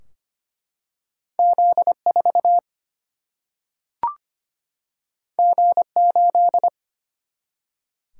Morse code, heard as a keyed tone, about 25 wpm, 700 Hz. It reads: Z 4 G 8
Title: Single-Letter Single-Number 25wpm